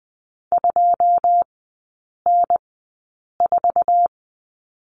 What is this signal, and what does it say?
Morse code, heard as a keyed tone, about 20 wpm, spaced as for 8 wpm, 700 Hz. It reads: 2N4
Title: 081 Lesson 14e - Number 2 - Three Characters 20wpm